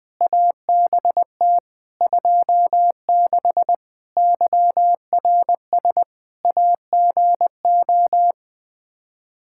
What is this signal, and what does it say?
Morse code, heard as a keyed tone, about 20 wpm, 700 Hz. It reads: ABT 26 YRS AGO
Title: Practice QSO - Exchange 3 - More Info - 20wpm